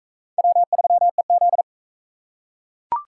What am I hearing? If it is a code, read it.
Morse code, heard as a keyed tone, about 38 wpm, 700 Hz. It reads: W3EZ